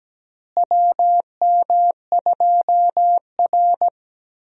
Morse code, heard as a keyed tone, about 17 wpm, 700 Hz. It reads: WM2R
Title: USA Call Signs - 4 Characters - Repeated 3x - 17wpm